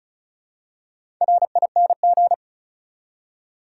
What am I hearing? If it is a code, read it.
Morse code, heard as a keyed tone, about 36 wpm, 700 Hz. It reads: RING
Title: Four Letter Words - 35wpm